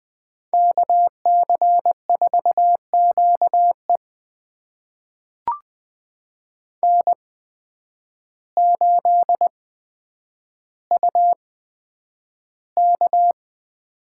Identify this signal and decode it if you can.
Morse code, heard as a keyed tone, about 20 wpm, spaced as for 9 wpm, 700 Hz. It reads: KC4QE N8UK
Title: USA Call Signs - 5 Characters - 8x Character Spacing - 20wpm